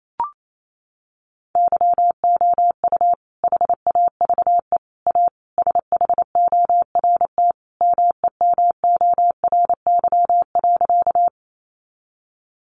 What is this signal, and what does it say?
Morse code, heard as a keyed tone, about 28 wpm, 700 Hz. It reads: YOU HAVE A SHORT MEMORY.
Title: Sentences from Top 1300 Words 28wpm